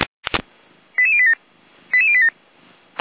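Five-tone selective call (ZVEI), heard as quick runs of five tones, tones of about 70 ms. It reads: 88087 88087
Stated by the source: ZVEI five-tone selective call 88087, received via SDR.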